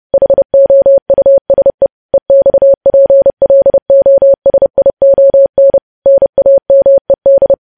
Morse code, heard as a keyed tone, about 30 wpm, 550 Hz. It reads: HOUSE EXPLOSION NAMED
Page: news headlines in morse code at 30 wpm